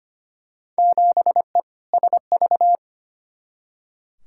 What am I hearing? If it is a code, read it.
Morse code, heard as a keyed tone, about 25 wpm, 700 Hz. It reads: 7E SV
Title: Field Day Exchanges Encore - 25wpm